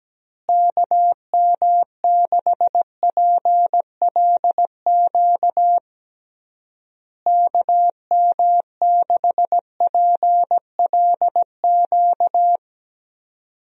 Morse code, heard as a keyed tone, about 17 wpm, 700 Hz. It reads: KM6PLQ KM6PLQ
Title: USA Call Signs - 6 Characters - Repeated 3x - 17wpm